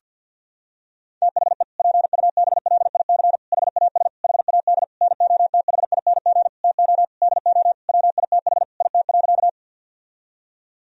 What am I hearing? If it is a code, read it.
Morse code, heard as a keyed tone, about 50 wpm, 700 Hz. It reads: THE PUBLIC HAS HAD NOTHING TO DO WITH IT.
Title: Sentences from Top 1000 Words - Review Part 3 - 50wpm